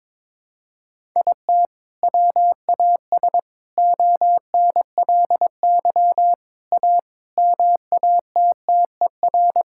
Morse code, heard as a keyed tone, about 22 wpm, 700 Hz. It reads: IT WAS ONLY A MATTER
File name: practice / 22wpm / Sets of 8 Words - 22wpm